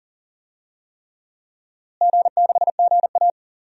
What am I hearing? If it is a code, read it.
Morse code, heard as a keyed tone, about 40 wpm, 700 Hz. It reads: G6GA
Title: DXCC-100 Call Signs - 4 Characters - Repeated 3x - 40wpm